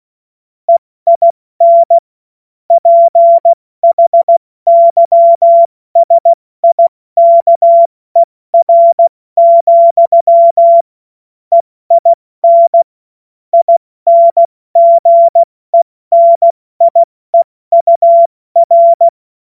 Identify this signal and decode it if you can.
Morse code, heard as a keyed tone, about 16 wpm, spaced as for 14 wpm, 680 Hz. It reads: EIN PHYSIKER, EIN INGENIEUR
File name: Morsekurs / Witze / Witz 07. Dezember 2021, 70 F80